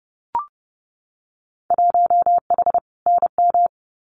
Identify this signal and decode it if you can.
Morse code, heard as a keyed tone, about 30 wpm, 700 Hz. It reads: 1H NM
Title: Winter Field Day Exchanges 30wpm